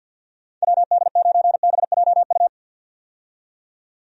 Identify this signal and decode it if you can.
Morse code, heard as a keyed tone, about 52 wpm, 700 Hz. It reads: WD9BJU
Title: USA Call Signs - 6 Characters - 50wpm